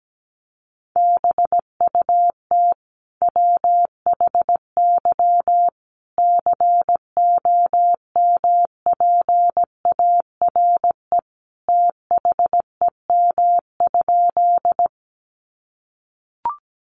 Morse code, heard as a keyed tone, about 17 wpm, 700 Hz. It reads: BUT WHY COMPARE THEM?
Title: Sentences from Top 1300 Words 17wpm